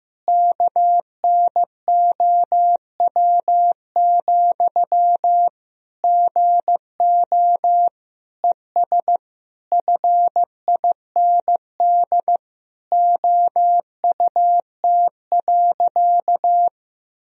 Morse code, heard as a keyed tone, about 16 wpm, 700 Hz. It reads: KNOW, GO ES FIND OUT.
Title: Sentences from Top 200 Words and Abbreviations 15wpm